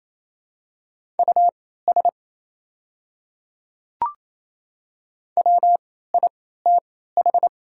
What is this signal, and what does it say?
Morse code, heard as a keyed tone, about 27 wpm, spaced as for 16 wpm, 700 Hz. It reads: US WITH
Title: 3x Character Spacing - Top 200 Words - 28wpm